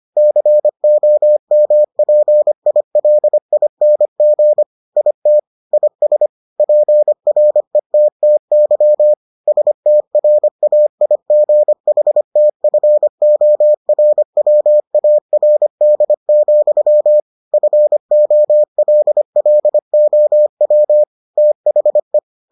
Morse code, heard as a keyed tone, about 25 wpm, 600 Hz. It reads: COMPILING IT IS PRETTY STRAIGHTFORWARD, FOLLOW THE